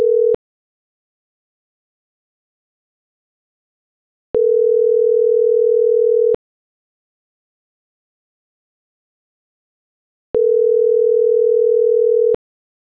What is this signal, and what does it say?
Ringback tone (North America), 440 + 480 Hz.